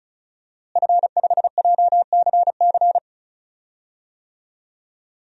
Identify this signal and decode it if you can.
Morse code, heard as a keyed tone, about 36 wpm, 700 Hz. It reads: F5JCC
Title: DXCC-100 Call Signs - 5 Characters - Repeated 3x - 35wpm